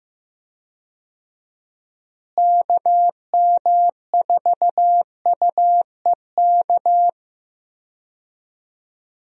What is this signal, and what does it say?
Morse code, heard as a keyed tone, about 15 wpm, 700 Hz. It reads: KM4UEK
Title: USA Call Signs - 6 Characters - Repeated 3x - 15wpm